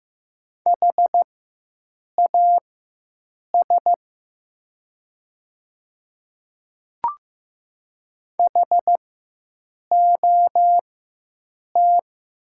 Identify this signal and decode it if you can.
Morse code, heard as a keyed tone, about 15 wpm, spaced as for 7 wpm, 700 Hz. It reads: HAS HOT
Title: 4x Character Spacing - Top 200 Words - 15wpm